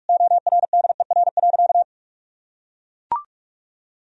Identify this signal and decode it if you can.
Morse code, heard as a keyed tone, about 42 wpm, 700 Hz. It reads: ORDER.
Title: Sentences from Top 1000 Words - Review Part 2 - 45wpm